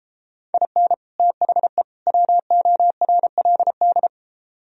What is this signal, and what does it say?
Morse code, heard as a keyed tone, about 33 wpm, 700 Hz. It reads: IN THE WORLD